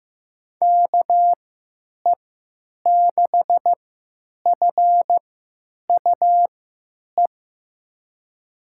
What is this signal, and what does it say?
Morse code, heard as a keyed tone, about 15 wpm, spaced as for 9 wpm, 700 Hz. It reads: KE6FUE
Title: USA Call Signs - 6 Characters - 3x Character Spacing - 15wpm